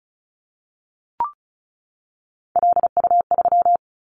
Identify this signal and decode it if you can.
Morse code, heard as a keyed tone, about 36 wpm, 690 Hz. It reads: LU3
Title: Callsign Prefixes for Top 50 DXCC - 35wpm